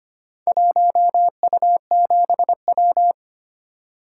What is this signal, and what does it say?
Morse code, heard as a keyed tone, about 25 wpm, 700 Hz. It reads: 1U7W
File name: practice / 25wpm / Four Letter-Number Combinations - 25wpm